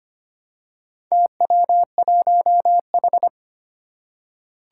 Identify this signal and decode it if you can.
Morse code, heard as a keyed tone, about 25 wpm, 700 Hz. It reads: TW1H
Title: Four Challenging Letter-Number Combinations 25wpm